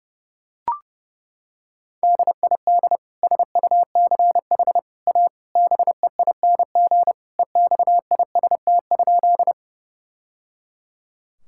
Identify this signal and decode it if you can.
Morse code, heard as a keyed tone, about 30 wpm, 700 Hz. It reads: DID SUCH A BEING EXIST?
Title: Sentences from Top 1000 Words - Review Part 5 - 30wpm